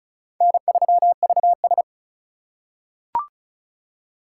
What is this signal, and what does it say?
Morse code, heard as a keyed tone, about 35 wpm, 700 Hz. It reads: N3VS
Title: USA Call Signs - 4 Characters - 4x Character Spacing - 35wpm